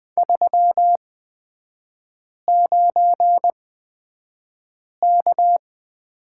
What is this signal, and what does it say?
Morse code, heard as a keyed tone, about 20 wpm, spaced as for 5 wpm, 700 Hz. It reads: 39K
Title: 243 Lesson 39e - Letter z - Three Characters 20wpm